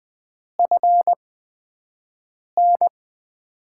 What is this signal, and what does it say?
Morse code, heard as a keyed tone, about 20 wpm, spaced as for 5 wpm, 700 Hz. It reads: FN
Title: USA Call Signs - 5 Characters - 8x Character Spacing - 20wpm